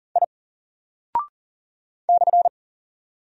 Morse code, heard as a keyed tone, about 40 wpm, 700 Hz. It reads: I /
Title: Single Character (Letter:Number:Punctuation:Prosign) Alt 40wpm